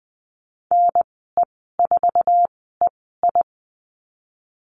Morse code, heard as a keyed tone, about 19 wpm, spaced as for 14 wpm, 700 Hz. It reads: NE4EI
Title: 055 Lesson 10g - Letter r - Call Signs 20wpm